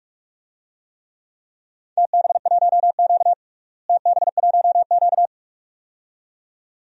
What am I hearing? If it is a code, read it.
Morse code, heard as a keyed tone, about 46 wpm, 700 Hz. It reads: TB1Q TB1Q
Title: DXCC-100 Call Signs - 4 Characters - Repeated 2x - 45wpm